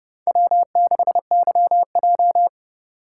Morse code, heard as a keyed tone, about 29 wpm, 700 Hz. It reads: W6YJ